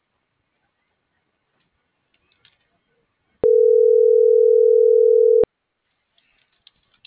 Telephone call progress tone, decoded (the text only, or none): ringback tone